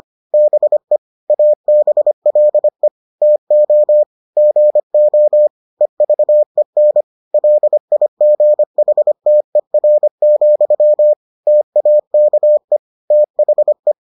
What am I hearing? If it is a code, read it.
Morse code, heard as a keyed tone, about 25 wpm, 600 Hz. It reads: BE ABLE TO GO EVEN LIGHTER, TAKE THE